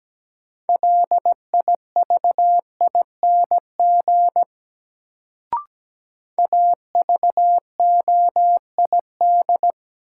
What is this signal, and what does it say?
Morse code, heard as a keyed tone, about 17 wpm, 700 Hz. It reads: LIVING AVOID
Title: Sentences from Top 1000 Words - Review Part 6 - New Word Drill - 17wpm